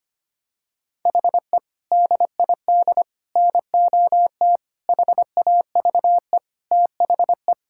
Morse code, heard as a keyed tone, about 25 wpm, 700 Hz. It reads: HE DID NOT HAVE THE
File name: practice / 25wpm / Sentences with 4-Word Phrases - 25wpm